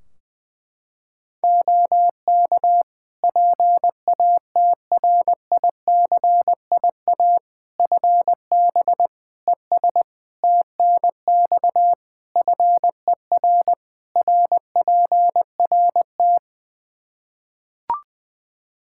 Morse code, heard as a keyed tone, about 20 wpm, 700 Hz. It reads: OK PATRICIA FB ES TNX FER RPRT
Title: Practice QSO - Exchange 2 - Info - 20wpm